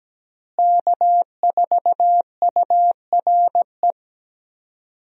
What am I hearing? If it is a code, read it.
Morse code, heard as a keyed tone, about 17 wpm, 700 Hz. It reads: K4URE
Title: USA Call Signs - 5 Characters - Repeated 3x - 17wpm